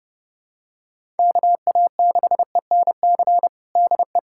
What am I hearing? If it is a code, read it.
Morse code, heard as a keyed tone, about 30 wpm, 690 Hz. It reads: KA6ENC DE
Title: USA Callsign DE Callsign - 6 Characters - 30wpm